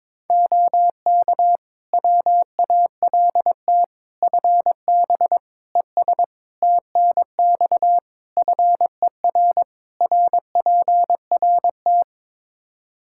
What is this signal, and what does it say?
Morse code, heard as a keyed tone, about 22 wpm, 700 Hz. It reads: OK WALT FB ES TNX FER RPRT
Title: Practice QSO - Exchange 2 - Info - 22wpm